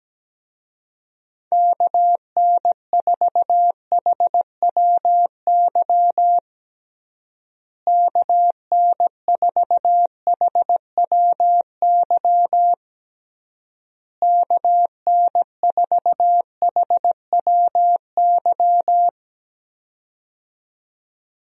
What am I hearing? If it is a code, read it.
Morse code, heard as a keyed tone, about 17 wpm, 700 Hz. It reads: KN4HWY KN4HWY KN4HWY
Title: USA Call Signs - 6 Characters - Repeated 3x - 17wpm